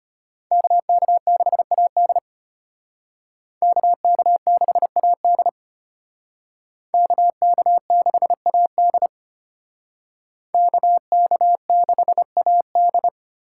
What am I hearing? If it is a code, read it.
Morse code, heard as a keyed tone, about 32 wpm, 700 Hz. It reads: KK6AD KK6AD KK6AD KK6AD